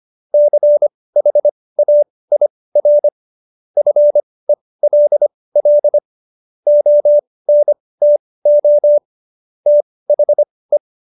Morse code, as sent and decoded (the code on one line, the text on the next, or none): -.-. .... .- .. .-. / ..-. .-.. .-.. / --- -. - --- / - .... .
CHAIR FELL ONTO THE